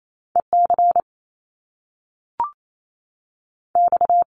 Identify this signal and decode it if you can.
Morse code, heard as a keyed tone, about 28 wpm, 700 Hz. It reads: E C X
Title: Top Two Letter Combinations in Words 28wpm